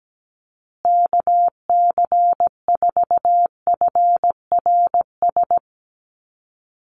Morse code, heard as a keyed tone, about 17 wpm, 700 Hz. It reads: KC4FRS
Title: USA Call Signs Repeated 3x 17wpm